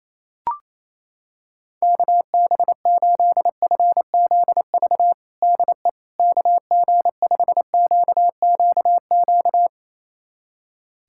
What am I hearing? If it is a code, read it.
Morse code, heard as a keyed tone, about 27 wpm, 700 Hz. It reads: KB8FZV DE KG5QQQ
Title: USA Callsign DE Callsign - 6 Characters - 28wpm